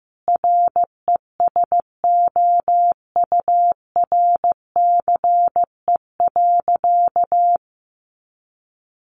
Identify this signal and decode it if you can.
Morse code, heard as a keyed tone, about 15 wpm, 700 Hz. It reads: RESOURCE.
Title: Sentences from Top 1300 Words 15wpm